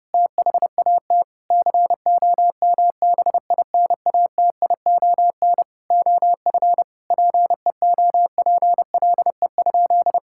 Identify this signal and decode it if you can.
Morse code, heard as a keyed tone, about 30 wpm, 700 Hz. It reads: THAT COMBINATION OF PEOPLE?
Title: Sentences from Top 1000 Words - Review Part 6 - 30wpm